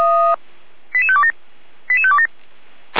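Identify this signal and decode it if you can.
Five-tone selective call (ZVEI), heard as quick runs of five tones, tones of about 70 ms. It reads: 88527 88527
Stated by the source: ZVEI five-tone selective call received on SDR.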